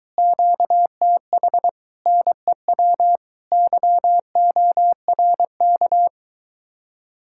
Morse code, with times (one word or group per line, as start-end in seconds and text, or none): 0.2-1.7 QTH
2.1-3.2 NEW
3.5-6.1 YORK